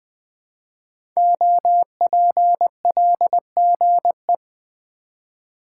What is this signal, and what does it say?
Morse code, heard as a keyed tone, about 20 wpm, 700 Hz. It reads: OPLGE
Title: Code Groups - 5 Letters - 20wpm